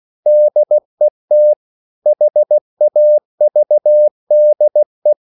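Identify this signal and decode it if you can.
Morse code, heard as a keyed tone, about 16 wpm, 600 Hz. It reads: DET HAVDE
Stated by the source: Slow (QRS) Morse code at 16 wpm, from CW med Gnister.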